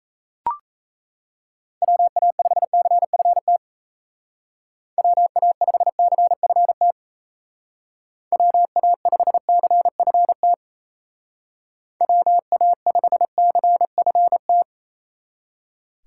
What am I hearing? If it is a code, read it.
Morse code, heard as a keyed tone, about 37 wpm, 700 Hz. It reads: WA5CFT WA5CFT WA5CFT WA5CFT